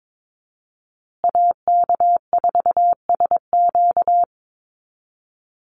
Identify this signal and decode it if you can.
Morse code, heard as a keyed tone, about 22 wpm, 700 Hz. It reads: AK4SQ
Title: Call Signs Repeated 2x - 22wpm